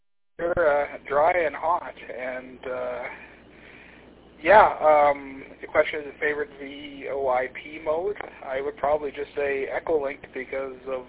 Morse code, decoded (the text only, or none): none